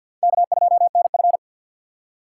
Morse code, heard as a keyed tone, about 62 wpm, 700 Hz. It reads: K2NF
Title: Speed Racing - USA Call Signs 50wpm